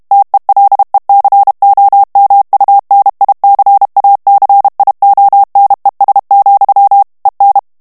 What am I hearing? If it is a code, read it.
Morse code, heard as a keyed tone, about 32 wpm, 800 Hz. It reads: TELECOMUNICACIONES, EN
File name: ESPAGNOL TEXTE EN CLAIR 2 VITESSE 30 MOTS MINUTES